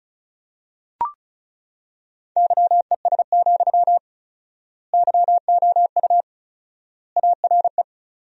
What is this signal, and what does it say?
Morse code, heard as a keyed tone, about 34 wpm, 700 Hz. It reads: YES, YOU ARE
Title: 4x Word-Spacing - Sentences from Top 500 Words - 35wpm